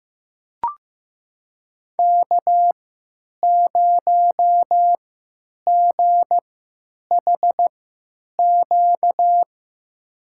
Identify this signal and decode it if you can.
Morse code, heard as a keyed tone, about 16 wpm, spaced as for 9 wpm, 700 Hz. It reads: K0GHQ
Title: USA Call Signs - 5 Characters - 3x Character Spacing - 15wpm